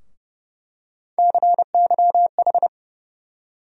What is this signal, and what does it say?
Morse code, heard as a keyed tone, about 32 wpm, 700 Hz. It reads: CYH
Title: USA Call Sign Postfixes - 3 Letters 30wpm